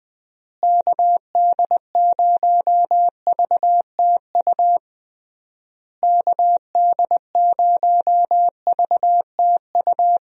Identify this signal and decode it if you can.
Morse code, heard as a keyed tone, about 20 wpm, 700 Hz. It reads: KD0VTU KD0VTU
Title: USA Call Signs - 6 Characters - Repeated 3x - 20wpm